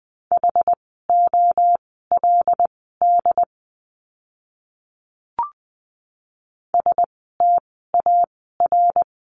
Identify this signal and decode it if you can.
Morse code, heard as a keyed tone, about 20 wpm, spaced as for 14 wpm, 700 Hz. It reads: HOLD STAR
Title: 101 Lesson 17f - Letter u - Words 20wpm